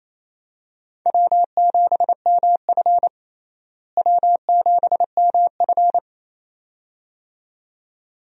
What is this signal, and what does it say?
Morse code, heard as a keyed tone, about 28 wpm, 700 Hz. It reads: W7MF W7MF
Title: USA Call Signs - 4 Characters - Repeated 2x - 28wpm